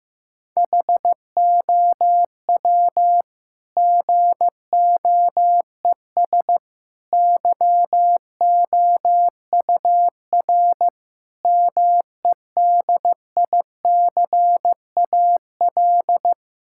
Morse code, heard as a keyed tone, about 15 wpm, 700 Hz. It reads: HOW GOES YOUR MEDICAL
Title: Sentences from Top 1000 Words - Review Part 5 - 15wpm